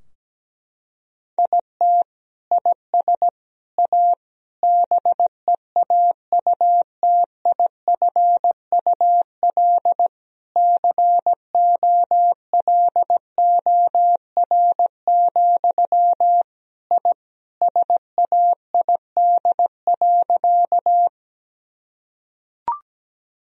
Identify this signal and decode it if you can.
Morse code, heard as a keyed tone, about 17 wpm, 700 Hz. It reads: IT IS A BEAUTIFUL COLOR, I SAID.
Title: Sentences from Top 1000 Words - Review Part 2 - 17wpm